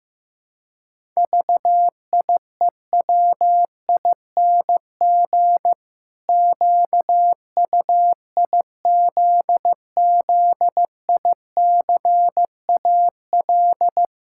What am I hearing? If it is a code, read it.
Morse code, heard as a keyed tone, about 15 wpm, 700 Hz. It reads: VIEWING QUIZZICAL
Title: Pangram Challenge v2 - 15wpm